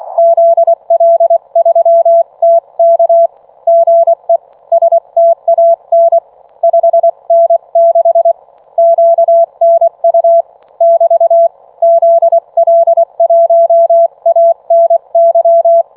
Morse code, heard as a keyed tone, about 24 wpm, 670 Hz. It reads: ZL3TK GE STAN 5N6 QNU <BT> ZL1ANY